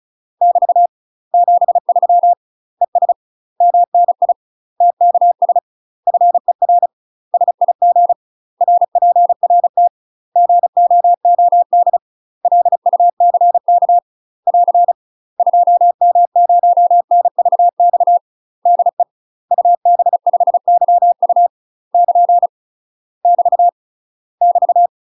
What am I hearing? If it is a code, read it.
Morse code, heard as a keyed tone, about 36 wpm, 700 Hz. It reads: <BT> 73 ES MNI TKS FER SIG RPRT GOOD LUCK <AR> 2M0NVX DE UB5YU <KN> <BT> <BT>
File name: UK MASTER QSOs TRAINING / 35WPM / UK MASTER QSOs 0031